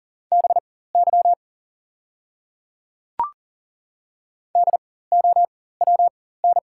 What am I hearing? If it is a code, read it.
Morse code, heard as a keyed tone, about 40 wpm, spaced as for 19 wpm, 700 Hz. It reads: BY DOWN
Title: 4x Character Spacing - Top 100 Words - 40wpm